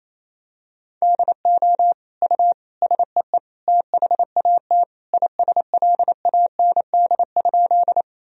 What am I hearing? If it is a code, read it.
Morse code, heard as a keyed tone, about 28 wpm, 700 Hz. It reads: DO U SEE THAT ISLAND?
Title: Sentences from Top 500 Words and Abbreviations 28wpm